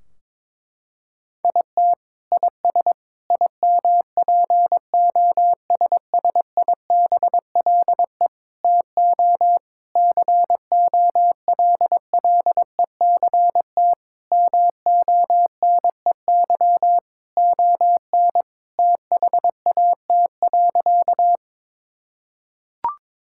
Morse code, heard as a keyed tone, about 22 wpm, 700 Hz. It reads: IT IS IMPOSSIBLE TO COLLECT MONEY ON THAT.
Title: Sentences from Top 1600 Words 22wpm